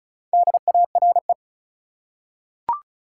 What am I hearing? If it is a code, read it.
Morse code, heard as a keyed tone, about 34 wpm, 700 Hz. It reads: DARE
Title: Four Letter Words - 35wpm